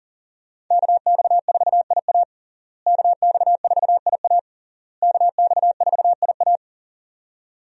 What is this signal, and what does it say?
Morse code, heard as a keyed tone, about 40 wpm, 700 Hz. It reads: KX4IA KX4IA KX4IA
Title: USA Call Signs - 5 Characters - Repeated 3x - 40wpm